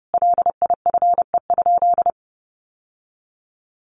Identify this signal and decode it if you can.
Morse code, heard as a keyed tone, about 30 wpm, 700 Hz. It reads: LIFE?
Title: Sentences from Top 1200 Words 30wpm